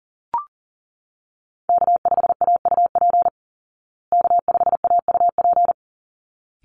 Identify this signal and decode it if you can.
Morse code, heard as a keyed tone, about 40 wpm, 700 Hz. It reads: K5AUP K5AUP
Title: Call Signs Repeated 3x - 40wpm